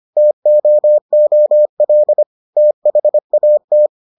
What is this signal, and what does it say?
Morse code, heard as a keyed tone, about 25 wpm, 600 Hz. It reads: TOOL THAT